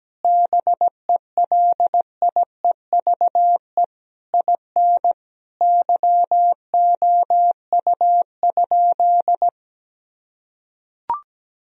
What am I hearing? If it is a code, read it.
Morse code, heard as a keyed tone, about 17 wpm, 700 Hz. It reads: BELIEVE IN YOU?
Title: Sentences from Top 1000 Words - Review Part 3 - 17wpm